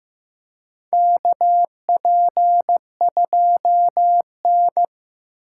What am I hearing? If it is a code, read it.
Morse code, heard as a keyed tone, about 15 wpm, 700 Hz. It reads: KP2N
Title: K1USN SST QSO Practice - 15wpm